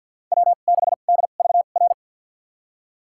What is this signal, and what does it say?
Morse code, heard as a keyed tone, about 50 wpm, spaced as for 36 wpm, 700 Hz. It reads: W6DVR
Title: USA Call Signs - 2x Character Spacing - 50wpm